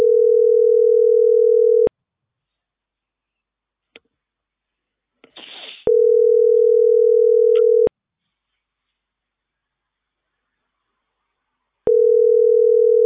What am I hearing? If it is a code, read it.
Ringback tone (North America), 440 + 480 Hz.